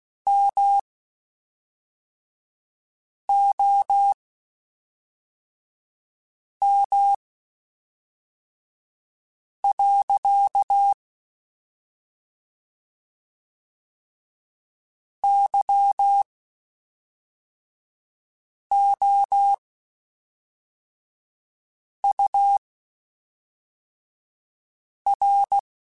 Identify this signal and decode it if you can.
Morse code, heard as a keyed tone, about 16 wpm, spaced as for 3 wpm, 800 Hz. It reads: MOM. YOUR
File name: (spaced) / QSOs Spaced-mp3s / WA7QOX 1 @5 sp